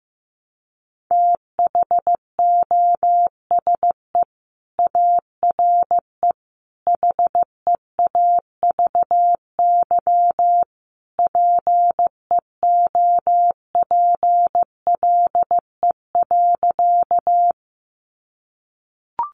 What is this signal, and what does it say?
Morse code, heard as a keyed tone, about 15 wpm, 700 Hz. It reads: THOSE ARE HEAVY PEOPLE.
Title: Sentences from Top 1100 Words 15wpm